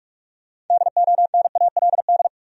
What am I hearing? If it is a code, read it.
Morse code, heard as a keyed tone, about 44 wpm, 700 Hz. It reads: DONALD
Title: USA Common Names - 45wpm